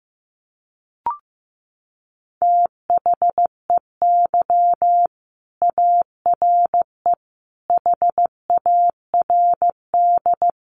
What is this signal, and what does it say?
Morse code, heard as a keyed tone, about 15 wpm, 700 Hz. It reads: THEY ARE HARD
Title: Sentences from Top 300 Words 15wpm